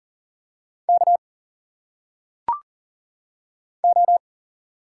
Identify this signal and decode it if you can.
Morse code, heard as a keyed tone, about 40 wpm, 700 Hz. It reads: K O
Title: Single Letters - Phonetic Pronunciation 40wpm